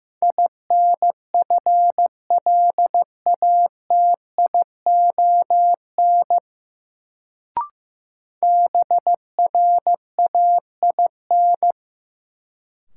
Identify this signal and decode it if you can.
Morse code, heard as a keyed tone, about 15 wpm, 700 Hz. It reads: INFLATION BRAIN
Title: Sentences from Top 1100 Words - New Word Drill Alt - 15wpm